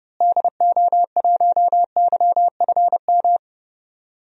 Morse code, heard as a keyed tone, about 31 wpm, 700 Hz. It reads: DO1YFM
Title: DXCC-100 Call Signs - 6 Characters - Repeated 2x - 30wpm